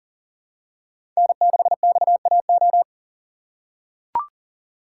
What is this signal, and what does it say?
Morse code, heard as a keyed tone, about 40 wpm, 700 Hz. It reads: N6XAO
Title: USA Call Signs - 5 Characters - 2x Character Spacing - 40wpm